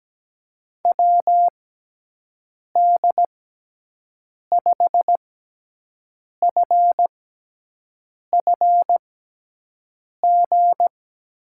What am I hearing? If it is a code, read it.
Morse code, heard as a keyed tone, about 17 wpm, spaced as for 6 wpm, 700 Hz. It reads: WD5FFG
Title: USA Call Signs - 6 Characters - 6x Character Spacing - 17wpm